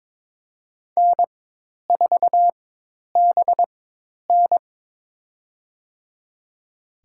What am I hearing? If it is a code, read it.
Morse code, heard as a keyed tone, about 23 wpm, spaced as for 10 wpm, 700 Hz. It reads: N4BN
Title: USA Call Signs - 4 Characters - 4x Character Spacing - 22wpm